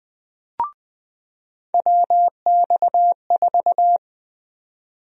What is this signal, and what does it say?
Morse code, heard as a keyed tone, about 20 wpm, 700 Hz. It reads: WX4
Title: USA Call Sign Prefixes - 20wpm